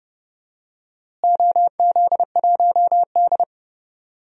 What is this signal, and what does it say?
Morse code, heard as a keyed tone, about 30 wpm, 700 Hz. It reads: OZ1D
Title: DXCC-100 Call Signs - 4 Characters - Repeated 2x - 30wpm